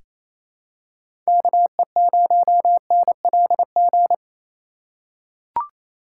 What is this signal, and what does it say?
Morse code, heard as a keyed tone, about 28 wpm, 700 Hz. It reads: KE0NLG
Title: USA Call Signs - 6 Characters - 2x Character Spacing - 28wpm